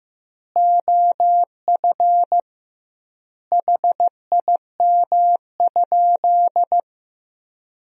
Morse code, heard as a keyed tone, about 15 wpm, 700 Hz. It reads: OF HIM?
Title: 2x Word-Spacing - Sentences from Top 200 Words - 15wpm